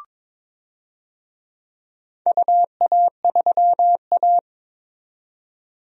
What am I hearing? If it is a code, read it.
Morse code, heard as a keyed tone, about 22 wpm, 700 Hz. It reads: UA3A
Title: DXCC-100 Call Signs - 4 Characters - 22wpm